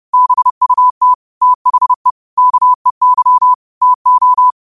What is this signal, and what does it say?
Morse code, heard as a keyed tone, about 30 wpm, 1000 Hz. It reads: BUT THE KEY TO
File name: qotdfiles / 30wpm